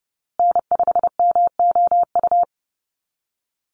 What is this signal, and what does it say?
Morse code, heard as a keyed tone, about 30 wpm, 700 Hz. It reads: N5MOU
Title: USA Call Signs Repeated 3x 30wpm